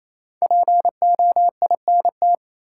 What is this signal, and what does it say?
Morse code, heard as a keyed tone, about 28 wpm, 700 Hz. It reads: POINT
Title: Sentences from Top 200 Words - New Word Drill - 28wpm